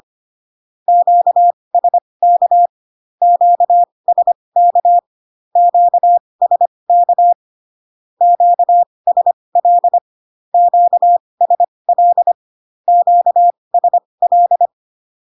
Morse code, heard as a keyed tone, about 25 wpm, spaced as for 20 wpm, 700 Hz. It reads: QSK QSK QSK QSL QSL QSL
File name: cw / Q-codes most common repeated 3 times 15 25wpm ik1hge